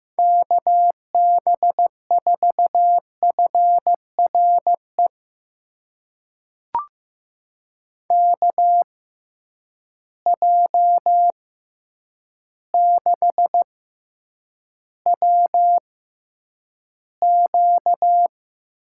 Morse code, heard as a keyed tone, about 15 wpm, spaced as for 8 wpm, 700 Hz. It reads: KB4FRE KJ6WQ
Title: USA Call Signs - 6x Character Spacing - 15wpm